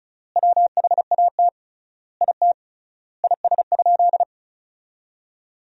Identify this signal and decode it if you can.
Morse code, heard as a keyed tone, about 36 wpm, 700 Hz. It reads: WHAT IT IS?
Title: 3x Word-Spacing - Sentences from Top 100 Words - 35wpm